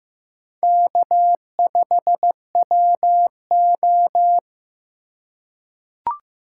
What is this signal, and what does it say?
Morse code, heard as a keyed tone, about 15 wpm, 700 Hz. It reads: K5WO
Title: USA Call Signs - 4x Character Spacing - 15wpm